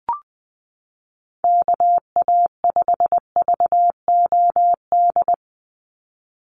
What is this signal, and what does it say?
Morse code, heard as a keyed tone, about 20 wpm, 700 Hz. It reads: KA5VOD
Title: USA Call Signs 20wpm